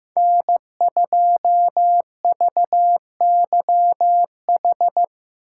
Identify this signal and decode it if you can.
Morse code, heard as a keyed tone, about 18 wpm, 700 Hz. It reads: N2VYH
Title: Speed Racing - USA 1x3 Call Signs 15wpm